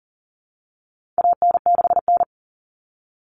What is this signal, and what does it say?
Morse code, heard as a keyed tone, about 40 wpm, 700 Hz. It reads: AN6N
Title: Common Cut Numbers - 4-Digit 40wpm